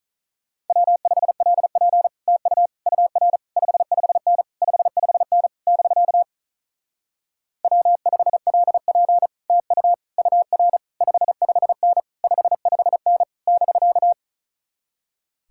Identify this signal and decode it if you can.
Morse code, heard as a keyed tone, about 46 wpm, 700 Hz. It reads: W5LP TU UR 55N 55N <BK> W5LP TU UR 55N 55N <BK>